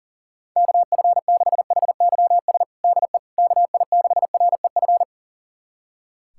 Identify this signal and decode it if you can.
Morse code, heard as a keyed tone, about 40 wpm, 700 Hz. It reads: KF6HYS DE KI6REF
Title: USA Callsign DE Callsign - 6 Characters - 40wpm